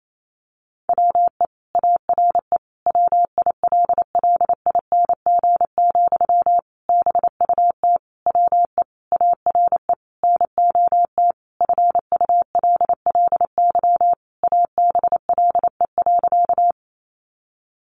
Morse code, heard as a keyed tone, about 28 wpm, 700 Hz. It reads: WE ARE WILLING, BUT WE ARE NOT FULLY ABLE.
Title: Sentences from Top 1300 Words 28wpm